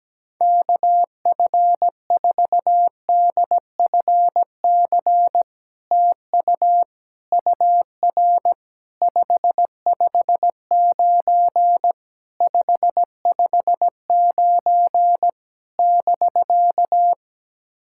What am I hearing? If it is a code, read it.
Morse code, heard as a keyed tone, about 17 wpm, 700 Hz. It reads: KF4DFC TU UR 559 559 <BK>
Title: SOTA QSO Practice 17wpm